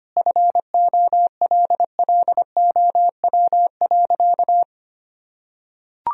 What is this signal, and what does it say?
Morse code, heard as a keyed tone, about 25 wpm, 700 Hz. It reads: FOLLOW.
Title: Sentences from Top 1000 Words - Review Part 3 - 25wpm